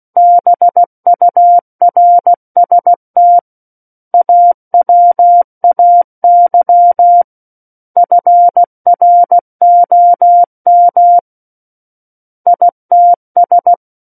Morse code, heard as a keyed tone, about 16 wpm, 700 Hz. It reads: BURST AWAY FROM ITS
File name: Book One / 07-Chapter 7 - 16 WPM